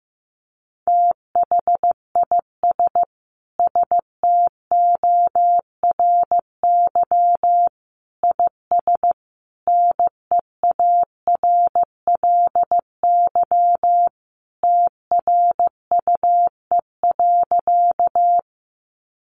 Morse code, heard as a keyed tone, about 15 wpm, 700 Hz. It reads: THIS STORY IS NEARLY TRUE.
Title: Sentences from Top 1300 Words 15wpm